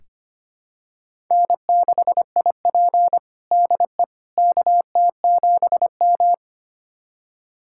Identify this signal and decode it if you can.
Morse code, heard as a keyed tone, about 25 wpm, 700 Hz. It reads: N6IP DE KT7M
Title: USA Callsign DE Callsign - 4 Characters - 25wpm